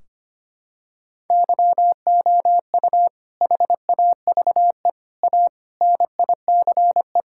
Morse code, heard as a keyed tone, about 25 wpm, 700 Hz. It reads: YOU HAVE A NICE
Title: Sentences from Top 1600 Words 25wpm